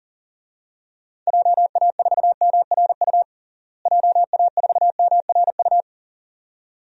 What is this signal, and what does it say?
Morse code, heard as a keyed tone, about 40 wpm, 700 Hz. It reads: JA4MRU JA4MRU
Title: DXCC-100 Call Signs - 6 Characters - Repeated 2x - 40wpm